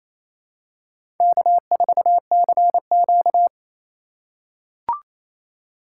Morse code, heard as a keyed tone, about 28 wpm, 700 Hz. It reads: K4CQ
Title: USA Call Signs - 4 Characters - 2x Character Spacing - 28wpm